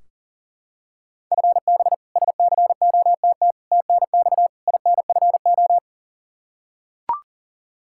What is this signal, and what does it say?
Morse code, heard as a keyed tone, about 40 wpm, 700 Hz. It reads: FB SCOTT TNX INFO
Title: K1USN SST QSO Practice 40wpm